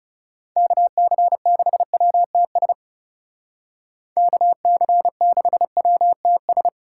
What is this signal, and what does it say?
Morse code, heard as a keyed tone, about 35 wpm, 700 Hz. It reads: KC6WTS KC6WTS
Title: Speed Racing - USA 2x3 Call Signs 30wpm